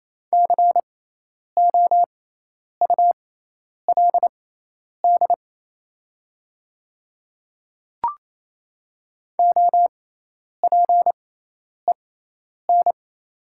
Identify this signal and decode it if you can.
Morse code, heard as a keyed tone, about 28 wpm, spaced as for 10 wpm, 690 Hz. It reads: COULD OPEN
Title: 6x Character Spacing - Top 500 Words - 28wpm